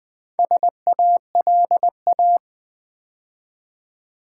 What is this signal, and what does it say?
Morse code, heard as a keyed tone, about 20 wpm, 700 Hz. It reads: SALA
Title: Build A Word - 2 Letters At A Time - 6-Letter Words - 20wpm